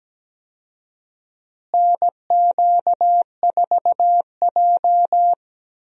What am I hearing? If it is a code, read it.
Morse code, heard as a keyed tone, about 17 wpm, 700 Hz. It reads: NQ4J
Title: USA Call Signs - 4 Characters - Repeated 3x - 17wpm